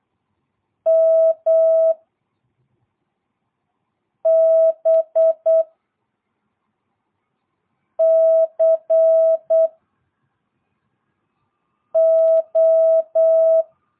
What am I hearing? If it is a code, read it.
Morse code, heard as a keyed tone, about 8 wpm, 660 Hz. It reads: M B C O
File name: jh8cbh / 2022cw